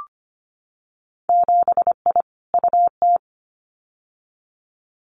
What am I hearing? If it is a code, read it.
Morse code, heard as a keyed tone, about 25 wpm, 700 Hz. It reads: 7I UT
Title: Winter Field Day Exchanges Encore 25wpm